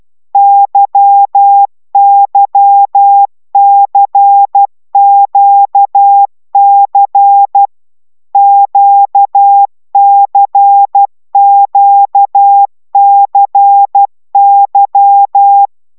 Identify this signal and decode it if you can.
Morse code, heard as a keyed tone, about 12 wpm, 800 Hz. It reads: YYCQC QCQCY
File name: taviro tanulas / cw 75 15 10